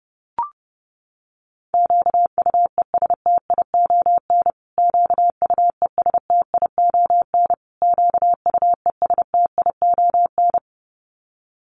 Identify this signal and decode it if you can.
Morse code, heard as a keyed tone, about 30 wpm, 700 Hz. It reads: QUESTION QUESTION QUESTION
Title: Top 500 Words Repeated 30wpm